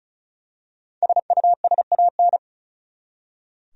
Morse code, heard as a keyed tone, about 34 wpm, 700 Hz. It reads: SUSAN
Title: USA Common Names - Max 5 Letters - 35wpm